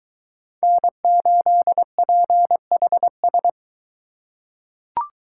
Morse code, heard as a keyed tone, about 22 wpm, 700 Hz. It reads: N8PHS